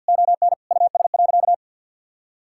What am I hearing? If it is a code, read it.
Morse code, heard as a keyed tone, about 50 wpm, 700 Hz. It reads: ON US.
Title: Sentences from Top 1000 Words - Review Part 6 - 50wpm